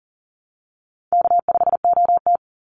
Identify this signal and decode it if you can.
Morse code, heard as a keyed tone, about 40 wpm, 700 Hz. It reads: K5OT
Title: Easy Call Signs Repeated 2x - 40wpm